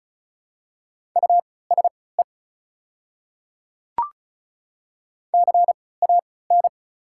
Morse code, heard as a keyed tone, about 36 wpm, spaced as for 20 wpm, 690 Hz. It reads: USE CAN
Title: 3x Character Spacing - Top 100 Words - 35wpm